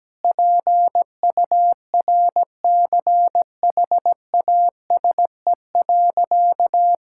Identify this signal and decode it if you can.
Morse code, heard as a keyed tone, about 17 wpm, 700 Hz. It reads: PURCHASE.
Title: Sentences from Top 1000 Words - Review Part 5 - 17wpm